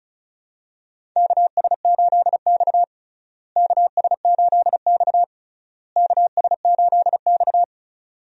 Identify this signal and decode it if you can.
Morse code, heard as a keyed tone, about 36 wpm, 700 Hz. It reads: KS8X KS8X KS8X
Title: USA Call Signs - 4 Characters - Repeated 3x - 35wpm